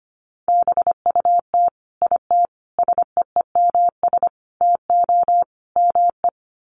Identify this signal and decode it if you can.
Morse code, heard as a keyed tone, about 25 wpm, 700 Hz. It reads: BUT IT SEEMS TO ME
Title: Sets of 5 Words Encore2 25wpm